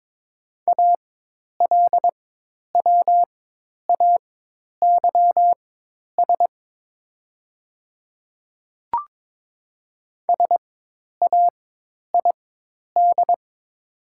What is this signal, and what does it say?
Morse code, heard as a keyed tone, about 22 wpm, spaced as for 10 wpm, 700 Hz. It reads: ALWAYS SAID
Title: 4x Character Spacing - Top 300 Words - 22wpm